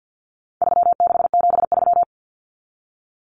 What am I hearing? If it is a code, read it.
Morse code, heard as a keyed tone, about 50 wpm, 700 Hz. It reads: 3673
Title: Common Cut Numbers - 4-Digit 50wpm